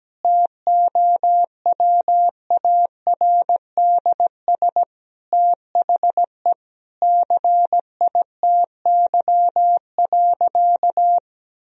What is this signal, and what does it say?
Morse code, heard as a keyed tone, about 17 wpm, 700 Hz. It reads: TOWARDS THE CITY.
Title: Sentences from Top 1000 Words - Review Part 2 - 17wpm